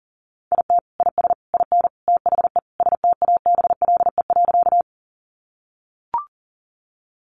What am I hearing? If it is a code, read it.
Morse code, heard as a keyed tone, about 40 wpm, 700 Hz. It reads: IT IS IN THE STABLE.
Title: Sentences from Top 1500 Words 40wpm